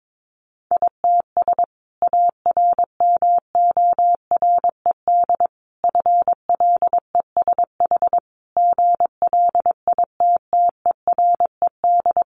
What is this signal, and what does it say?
Morse code, heard as a keyed tone, about 22 wpm, 700 Hz. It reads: ITS ARMORED FLESH GLITTERED
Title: A Sound Of Thunder 22wpm